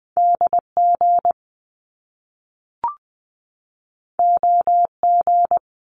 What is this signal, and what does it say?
Morse code, heard as a keyed tone, about 20 wpm, 700 Hz. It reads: DG OG
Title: Top Two Letter Combinations in Words 20wpm